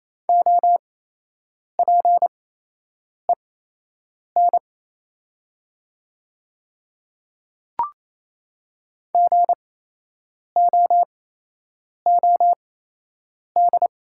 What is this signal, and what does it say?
Morse code, heard as a keyed tone, about 27 wpm, spaced as for 8 wpm, 690 Hz. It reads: OPEN GOOD
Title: 8x Character Spacing - Top 300 Words - 28wpm